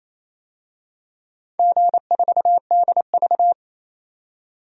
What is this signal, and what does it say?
Morse code, heard as a keyed tone, about 28 wpm, 700 Hz. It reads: G4DV
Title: DXCC-100 Call Signs - 4 Characters - 28wpm